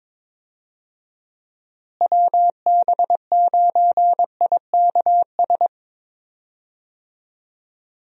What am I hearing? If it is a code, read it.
Morse code, heard as a keyed tone, about 22 wpm, 700 Hz. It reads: WB9IKS
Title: USA Call Signs - 6 Characters - Repeated 2x - 22wpm